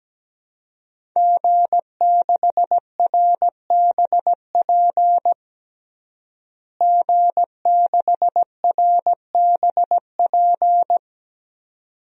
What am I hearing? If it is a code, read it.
Morse code, heard as a keyed tone, about 17 wpm, 700 Hz. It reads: G6RBP G6RBP
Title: DXCC-100 Call Signs - 5 Characters - Repeated 2x - 17wpm